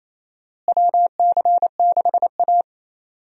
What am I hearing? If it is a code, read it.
Morse code, heard as a keyed tone, about 28 wpm, 700 Hz. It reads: WC6A
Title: USA Call Signs - 4 Characters - Repeated 3x - 28wpm